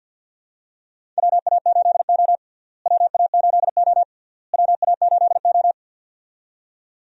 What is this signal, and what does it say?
Morse code, heard as a keyed tone, about 47 wpm, 700 Hz. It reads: WA8O WA8O WA8O
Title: USA Call Signs - 4 Characters - Repeated 3x - 50wpm